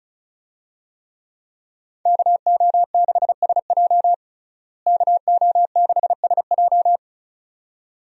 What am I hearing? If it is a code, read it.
Morse code, heard as a keyed tone, about 34 wpm, 700 Hz. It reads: KO6SJ KO6SJ
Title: USA Call Signs - 5 Characters - Repeated 2x - 35wpm